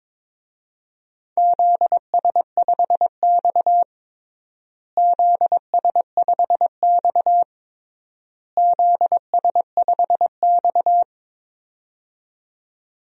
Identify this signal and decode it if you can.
Morse code, heard as a keyed tone, about 22 wpm, 700 Hz. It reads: ZS5X ZS5X ZS5X
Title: DXCC-100 Call Signs - 4 Characters - Repeated 3x - 22wpm